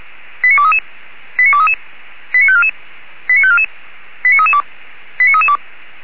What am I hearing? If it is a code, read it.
Five-tone selective call (ZVEI), heard as quick runs of five tones, tones of about 70 ms. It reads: 89233 89233 89544 89544 89332 89332